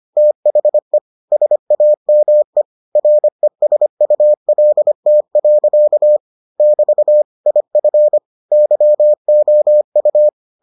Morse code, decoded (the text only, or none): THE SAME RESULT. <BT> IF YOU